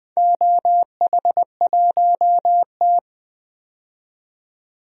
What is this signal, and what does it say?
Morse code, heard as a keyed tone, about 20 wpm, 700 Hz. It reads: OH1T
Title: DXCC-100 Call Signs - 4 Characters - 20wpm